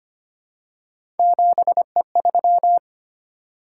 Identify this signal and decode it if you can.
Morse code, heard as a keyed tone, about 25 wpm, 700 Hz. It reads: 7E3
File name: Three Letter-Number Combinations - 25wpm